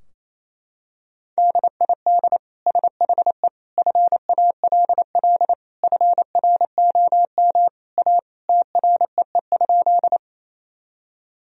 Morse code, heard as a keyed tone, about 28 wpm, 700 Hz. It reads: DID SHE FALL FROM A TREE?
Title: Sentences from Top 1000 Words - Review Part 2 - 28wpm